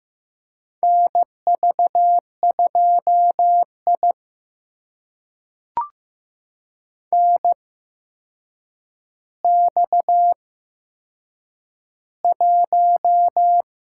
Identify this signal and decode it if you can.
Morse code, heard as a keyed tone, about 15 wpm, spaced as for 7 wpm, 700 Hz. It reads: NV2I NX1
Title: USA Call Signs - 4 Characters - 8x Character Spacing - 15wpm